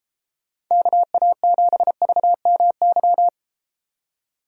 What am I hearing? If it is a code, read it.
Morse code, heard as a keyed tone, about 33 wpm, 700 Hz. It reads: KA7VMY